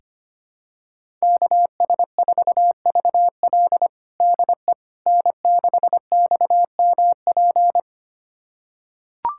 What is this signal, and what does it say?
Morse code, heard as a keyed tone, about 25 wpm, 700 Hz. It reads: KS4VL DE N6XMP
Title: USA Callsign DE Callsign - 5 Characters - 25wpm